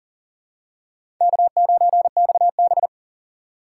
Morse code, heard as a keyed tone, about 37 wpm, 700 Hz. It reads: K9XB
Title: USA Call Signs - 4 Characters - 6x Character Spacing - 40wpm